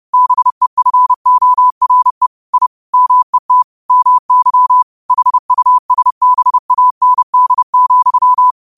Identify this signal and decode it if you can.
Morse code, heard as a keyed tone, about 30 wpm, 1000 Hz. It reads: BEFORE I MET MY HUSBAND,
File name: qotdfiles / 30wpm